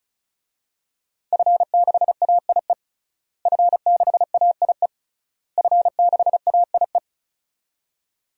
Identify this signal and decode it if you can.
Morse code, heard as a keyed tone, about 36 wpm, 700 Hz. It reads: F6AIE F6AIE F6AIE
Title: DXCC-100 Call Signs - 5 Characters - Repeated 3x - 35wpm